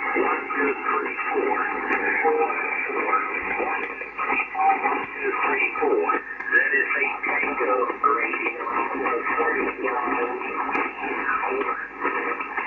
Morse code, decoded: none